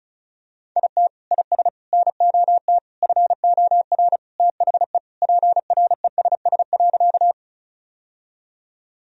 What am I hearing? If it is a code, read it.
Morse code, heard as a keyed tone, about 36 wpm, 700 Hz. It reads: IT IS NOT FOR THE PRESS.
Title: Sentences from Top 1000 Words - Review Part 1 - 35wpm